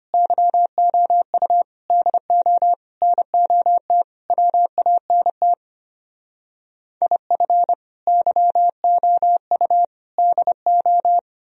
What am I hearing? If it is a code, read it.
Morse code, heard as a keyed tone, about 31 wpm, 700 Hz. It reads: YOU DO NOT WANT IF YOU DO
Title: Speed-Racing - Sets of 5 Words 25wpm